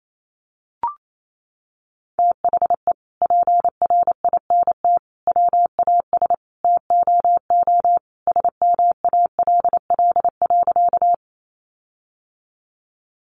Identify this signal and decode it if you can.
Morse code, heard as a keyed tone, about 28 wpm, 700 Hz. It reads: THE PRINT WAS TOO SMALL.
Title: Sentences from Top 1400 Words 28wpm